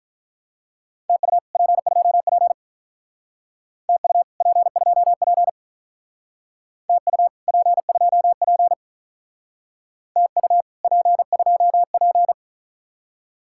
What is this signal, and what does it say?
Morse code, heard as a keyed tone, about 46 wpm, 700 Hz. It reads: TU P2P TU P2P TU P2P TU P2P